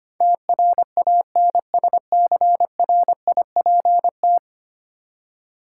Morse code, heard as a keyed tone, about 25 wpm, 700 Hz. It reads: TRANSCRIPT
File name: 10 Letter Words - First 3 Letters Given - 25wpm